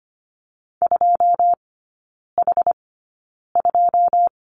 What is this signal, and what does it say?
Morse code, heard as a keyed tone, about 25 wpm, spaced as for 9 wpm, 690 Hz. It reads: 2H2
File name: Lesson 14e - Number 2 - Three Characters 25wpm